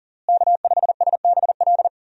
Morse code, heard as a keyed tone, about 37 wpm, 700 Hz. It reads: K5SBL
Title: USA Call Signs - 5 Characters - Repeated 2x - 40wpm